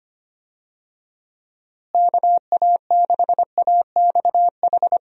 Morse code, heard as a keyed tone, about 25 wpm, 700 Hz. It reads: KA6AXH
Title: USA Call Signs - 6 Characters - Repeated 3x - 25wpm